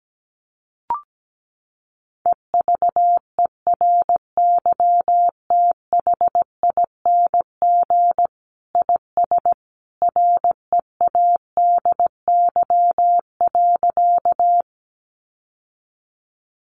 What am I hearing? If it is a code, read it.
Morse code, heard as a keyed tone, about 17 wpm, 700 Hz. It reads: EVERYTHING IS READY.
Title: Sentences from Top 700 Words 17wpm